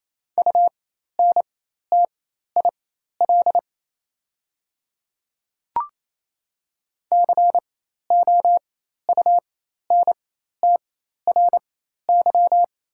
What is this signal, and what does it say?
Morse code, heard as a keyed tone, about 28 wpm, spaced as for 13 wpm, 700 Hz. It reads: UNTIL COUNTRY
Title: 4x Character Spacing - Top 300 Words - 28wpm